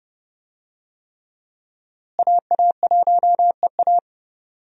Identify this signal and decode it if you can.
Morse code, heard as a keyed tone, about 30 wpm, 700 Hz. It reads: AA1EA
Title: USA Call Signs - 5 Characters - Repeated 3x - 30wpm